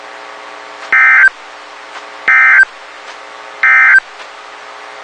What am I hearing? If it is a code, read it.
Emergency Alert System (SAME) tones, heard as harsh buzzing bursts. It is heard from 0.8 to 4.1 s.
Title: WXK91 NOAA Weather Radio